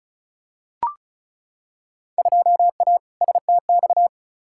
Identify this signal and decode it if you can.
Morse code, heard as a keyed tone, about 36 wpm, 700 Hz. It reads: 2A STX
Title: Field Day Exchanges - 35wpm